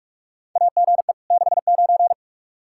Morse code, heard as a keyed tone, about 47 wpm, 700 Hz. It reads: AGE 69
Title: Practice QSO - Exchange 3 - More Info - 45wpm